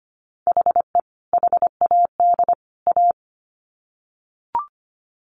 Morse code, heard as a keyed tone, about 25 wpm, 700 Hz. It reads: HE HAD A
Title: Sets of 3 Words 25wpm